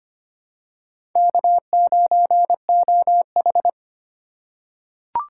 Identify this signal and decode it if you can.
Morse code, heard as a keyed tone, about 25 wpm, 700 Hz. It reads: K9OH
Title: USA Call Signs - 4 Characters - 3x Character Spacing - 25wpm